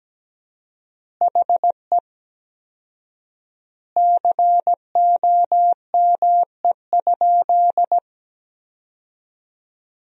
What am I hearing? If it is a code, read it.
Morse code, heard as a keyed tone, about 17 wpm, 700 Hz. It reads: HE COME?
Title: 4x Word-Spacing - Sentences from Top 200 Words - 17wpm